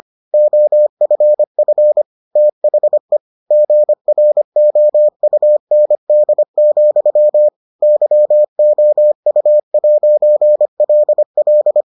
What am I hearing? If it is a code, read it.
Morse code, heard as a keyed tone, about 25 wpm, 600 Hz. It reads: OFF THE GROUND, YOU'LL